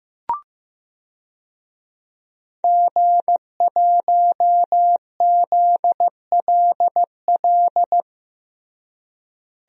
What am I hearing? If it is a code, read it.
Morse code, heard as a keyed tone, about 15 wpm, 700 Hz. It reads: G1ZLL
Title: DXCC-100 Call Signs - 5 Characters - Repeated 3x - 15wpm